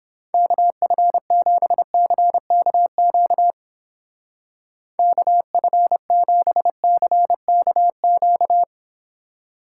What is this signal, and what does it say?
Morse code, heard as a keyed tone, about 29 wpm, 700 Hz. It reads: KF7CKQ KF7CKQ